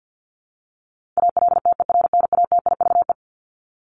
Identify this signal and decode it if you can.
Morse code, heard as a keyed tone, about 50 wpm, 700 Hz. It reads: ALTERNATIVE
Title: Top 501-750 Words 50wpm